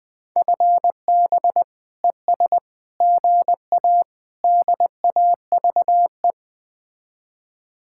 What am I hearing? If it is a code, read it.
Morse code, heard as a keyed tone, about 20 wpm, 700 Hz. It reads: FB ES GA DAVE
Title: Speed-Racing Common QSO Phrases - 17wpm